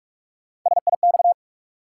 Morse code, heard as a keyed tone, about 46 wpm, 700 Hz. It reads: SIX
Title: Sentences from Top 600 Words - New Word Drill - 45wpm